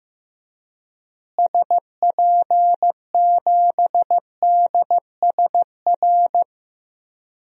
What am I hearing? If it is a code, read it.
Morse code, heard as a keyed tone, about 15 wpm, 700 Hz. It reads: SP7DSR
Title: DXCC-100 Call Signs - 6 Characters - Repeated 3x - 15wpm